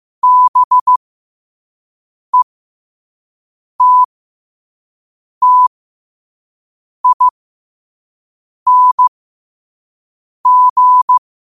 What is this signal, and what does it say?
Morse code, heard as a keyed tone, about 15 wpm, spaced as for 5 wpm, 1000 Hz. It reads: BETTING
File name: qotdfiles / 5wpm